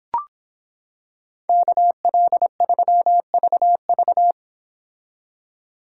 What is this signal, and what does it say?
Morse code, heard as a keyed tone, about 26 wpm, 700 Hz. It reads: KL3VV